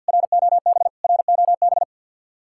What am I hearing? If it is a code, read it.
Morse code, heard as a keyed tone, about 66 wpm, spaced as for 58 wpm, 700 Hz. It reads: ROB ROB
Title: Speed-Racing Common QSO Phrases Encore - 50wpm